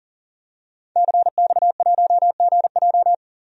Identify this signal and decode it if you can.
Morse code, heard as a keyed tone, about 40 wpm, 700 Hz. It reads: CX1GJ
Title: DXCC-100 Call Signs - 5 Characters - 40wpm